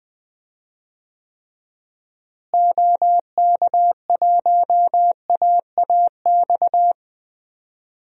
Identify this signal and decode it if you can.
Morse code, heard as a keyed tone, about 20 wpm, 700 Hz. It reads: OK1AAX
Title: DXCC-100 Call Signs - 6 Characters - Repeated 2x - 20wpm